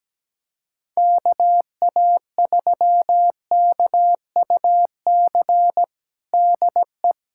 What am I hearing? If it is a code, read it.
Morse code, heard as a keyed tone, about 17 wpm, 700 Hz. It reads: KA3KUC DE
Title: USA Callsign DE Callsign 17wpm